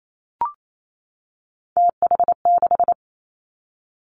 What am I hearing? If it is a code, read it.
Morse code, heard as a keyed tone, about 28 wpm, 690 Hz. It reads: TH6
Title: Three Challenging Letter-Number Combinations 28wpm